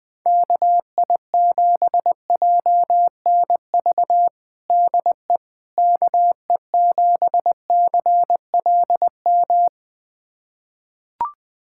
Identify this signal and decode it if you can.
Morse code, heard as a keyed tone, about 20 wpm, 700 Hz. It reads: KI7JNV DE KE7CLM
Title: USA Callsign DE Callsign - 6 Characters - 20wpm